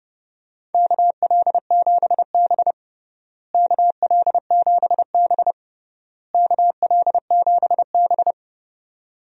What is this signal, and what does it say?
Morse code, heard as a keyed tone, about 30 wpm, 700 Hz. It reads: KL7B KL7B KL7B
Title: DXCC-100 Call Signs - 4 Characters - Repeated 3x - 30wpm